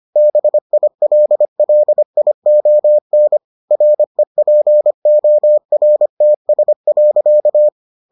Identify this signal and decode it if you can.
Morse code, heard as a keyed tone, about 25 wpm, 600 Hz. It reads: BILLION REPORTS.